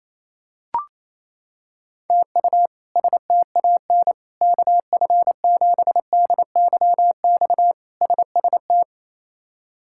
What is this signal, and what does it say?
Morse code, heard as a keyed tone, about 28 wpm, 700 Hz. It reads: TU STAN KF7DYX SST
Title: K1USN SST QSO Practice - 28wpm